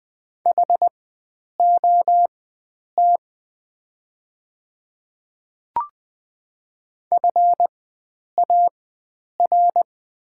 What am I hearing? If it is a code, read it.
Morse code, heard as a keyed tone, about 20 wpm, spaced as for 9 wpm, 700 Hz. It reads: HOT FAR
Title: 4x Character Spacing - Top 300 Words - 20wpm